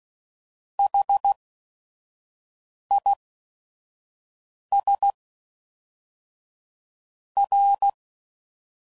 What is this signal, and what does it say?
Morse code, heard as a keyed tone, about 16 wpm, spaced as for 5 wpm, 800 Hz. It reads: HIS R
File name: CW Code Groups @5WPM / Technical Subjects / XC